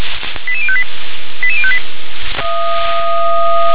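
Five-tone selective call (ZVEI), heard as quick runs of five tones, tones of about 70 ms. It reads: 88058 88058
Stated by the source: Received on an SDR, ZVEI five-tone selective call.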